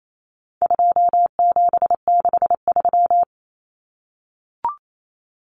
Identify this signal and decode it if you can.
Morse code, heard as a keyed tone, about 28 wpm, 700 Hz. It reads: 2763
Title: Common Cut Numbers - 4-Digit 28wpm